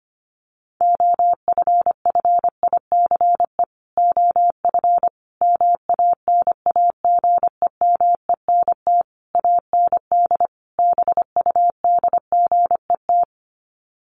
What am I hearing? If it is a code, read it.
Morse code, heard as a keyed tone, about 25 wpm, 700 Hz. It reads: OFFICE OF MANAGEMENT AND BUDGET
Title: Sets of 5 Words Encore2 25wpm